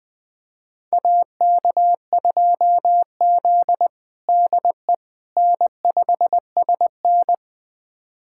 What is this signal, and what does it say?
Morse code, heard as a keyed tone, about 20 wpm, 700 Hz. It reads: AK2Z DE N5SN
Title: USA Callsign DE Callsign - 4 Characters - 20wpm